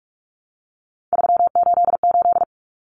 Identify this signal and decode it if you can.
Morse code, heard as a keyed tone, about 44 wpm, 700 Hz. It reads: 388
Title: Common Cut Numbers - 3-Digit 45wpm